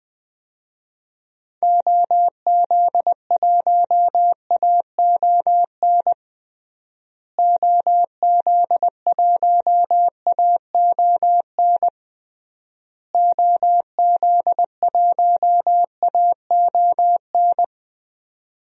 Morse code, heard as a keyed tone, about 20 wpm, 700 Hz. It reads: OZ1AON OZ1AON OZ1AON
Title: DXCC-100 Call Signs - 6 Characters - Repeated 3x - 20wpm